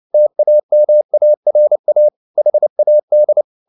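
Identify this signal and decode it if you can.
Morse code, heard as a keyed tone, about 29 wpm, 600 Hz. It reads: TAMARA HAD
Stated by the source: Morse code at 29 wpm, sending Ken Follett's novel Never in English, from CW med Gnister.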